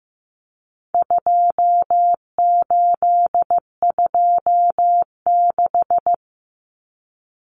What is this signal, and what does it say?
Morse code, heard as a keyed tone, about 15 wpm, 700 Hz. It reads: 2826
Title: Numbers - 4-Digit 15wpm